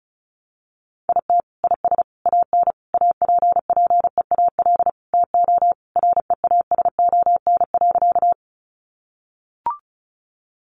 Morse code, heard as a keyed tone, about 34 wpm, 700 Hz. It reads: IT IS AN APPEAL TO REASON.
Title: Sentences from Top 1300 Words 35wpm